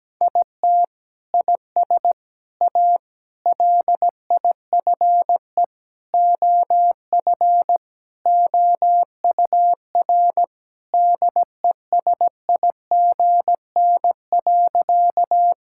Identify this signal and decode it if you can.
Morse code, heard as a keyed tone, about 17 wpm, 700 Hz. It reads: IT IS A LIFE OF OUR DESIGN.
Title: Sentences from Top 1000 Words - Review Part 5 - 17wpm